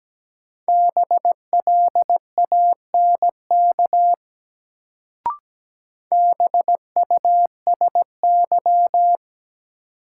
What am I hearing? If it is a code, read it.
Morse code, heard as a keyed tone, about 17 wpm, 700 Hz. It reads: BLANK BUSY
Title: Sentences from Top 1500 Words - New Word Drill Alt - 17wpm